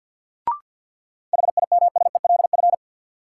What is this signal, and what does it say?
Morse code, heard as a keyed tone, about 50 wpm, 700 Hz. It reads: HIMSELF
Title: Sentences from Top 1500 Words - New Word Drill - 50wpm